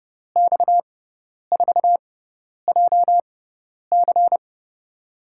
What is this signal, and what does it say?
Morse code, heard as a keyed tone, about 32 wpm, spaced as for 10 wpm, 700 Hz. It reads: X4JC
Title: USA Call Signs - 6x Character Spacing - 30wpm